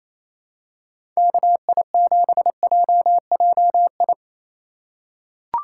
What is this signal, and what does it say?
Morse code, heard as a keyed tone, about 28 wpm, 690 Hz. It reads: KI7JJI
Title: USA Call Signs - 2x Character Spacing - 28wpm